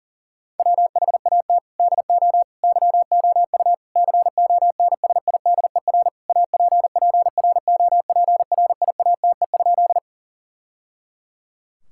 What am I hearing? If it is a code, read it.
Morse code, heard as a keyed tone, about 40 wpm, 690 Hz. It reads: WHAT DO YOU CONSIDER APPROPRIATE?
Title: Sentences from Top 1000 Words - Review Part 5 - 40wpm